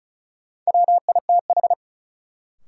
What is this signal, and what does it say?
Morse code, heard as a keyed tone, about 34 wpm, 700 Hz. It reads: WITH
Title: Sentences from Top 100 Words - New Word Drill - 35wpm